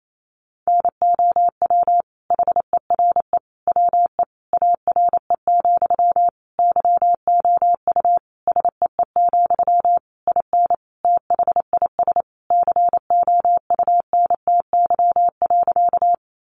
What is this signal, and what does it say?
Morse code, heard as a keyed tone, about 28 wpm, 700 Hz. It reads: NOW HERE WE ARE, YOU SEE, IN THIS COUNTY.
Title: Sentences from Top 1200 Words 28wpm